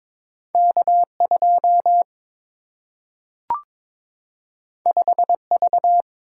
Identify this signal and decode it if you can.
Morse code, heard as a keyed tone, about 22 wpm, 700 Hz. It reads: K2 5V
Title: Single-Letter Single-Number 22wpm